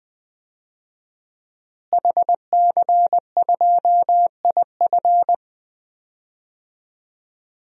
Morse code, heard as a keyed tone, about 20 wpm, 700 Hz. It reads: HC2IF
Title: DXCC-100 Call Signs - 5 Characters - Repeated 2x - 20wpm